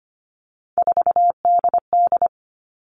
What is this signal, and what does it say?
Morse code, heard as a keyed tone, about 25 wpm, 700 Hz. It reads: 4DD
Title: All Cut Numbers - 3-Digit 25wpm